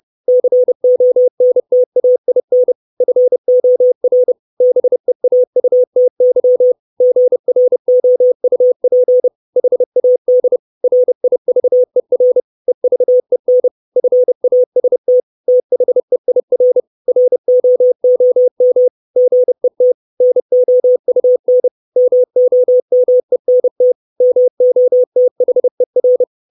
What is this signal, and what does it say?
Morse code, heard as a keyed tone, about 30 wpm, 500 Hz. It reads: CONTAIN FOR BEAUTY GROUP HAD RIVER EVEN FAST THEIR ROOM GET NOUN MOMENT MOTHER
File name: 632 Words 30WPM